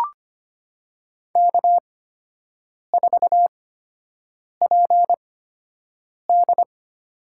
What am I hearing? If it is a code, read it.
Morse code, heard as a keyed tone, about 25 wpm, spaced as for 7 wpm, 700 Hz. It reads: K4PD
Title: USA Call Signs - 4 Characters - 8x Character Spacing - 25wpm